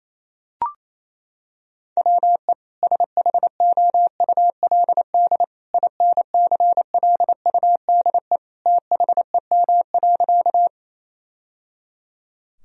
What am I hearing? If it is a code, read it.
Morse code, heard as a keyed tone, about 28 wpm, 700 Hz. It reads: WE SHOULD INCLUDE THEM.
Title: Sentences from Top 1000 Words - Review Part 3 - 28wpm